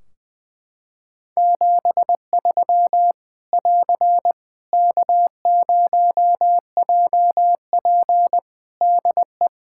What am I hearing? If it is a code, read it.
Morse code, heard as a keyed tone, about 20 wpm, 700 Hz. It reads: 73 <AR> K0JP DE
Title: Practice QSO - Exchange 4 - Ending - 20wpm